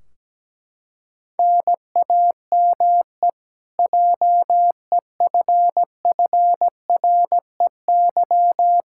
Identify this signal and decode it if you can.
Morse code, heard as a keyed tone, about 17 wpm, 700 Hz. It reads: NAME JEFFREY
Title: Practice QSO - Exchange 1 - Beginning - 17wpm